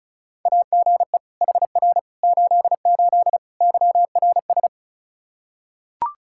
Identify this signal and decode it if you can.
Morse code, heard as a keyed tone, about 34 wpm, 700 Hz. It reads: AGE HR 88 YRS
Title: Practice QSO - Exchange 3 - More Info - 35wpm